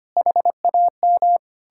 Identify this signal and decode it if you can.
Morse code, heard as a keyed tone, about 33 wpm, 700 Hz. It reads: HAM